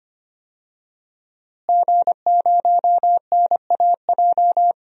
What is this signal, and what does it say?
Morse code, heard as a keyed tone, about 25 wpm, 700 Hz. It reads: G0NAJ
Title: DXCC-100 Call Signs - 5 Characters - Repeated 2x - 25wpm